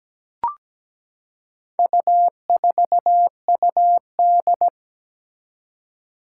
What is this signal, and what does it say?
Morse code, heard as a keyed tone, about 17 wpm, 700 Hz. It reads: U4UD
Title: Four Challenging Letter-Number Combinations 17wpm